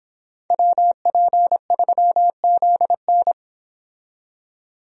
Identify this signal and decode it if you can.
Morse code, heard as a keyed tone, about 29 wpm, 700 Hz. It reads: WP3ZN